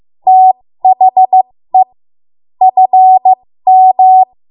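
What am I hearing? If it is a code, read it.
Morse code, heard as a keyed tone, about 15 wpm, spaced as for 13 wpm, 750 Hz. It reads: THE FM
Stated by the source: 13 wpm Morse code practice archive transmission.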